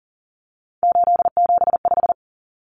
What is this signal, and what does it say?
Morse code, heard as a keyed tone, about 40 wpm, 700 Hz. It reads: 875
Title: Numbers - 3-Digit 40wpm